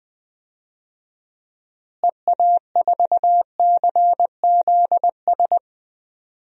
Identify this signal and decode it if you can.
Morse code, heard as a keyed tone, about 20 wpm, 700 Hz. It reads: EA4CZS
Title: DXCC-100 Call Signs - 6 Characters - Repeated 3x - 20wpm